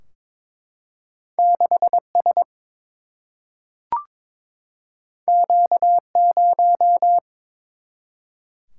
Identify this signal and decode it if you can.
Morse code, heard as a keyed tone, about 22 wpm, 700 Hz. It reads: 6S Q0
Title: Single-Letter Single-Number 22wpm